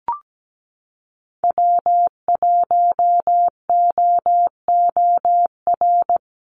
Morse code, heard as a keyed tone, about 17 wpm, 700 Hz. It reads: W1OOR
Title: USA Call Signs Repeated 2x 17wpm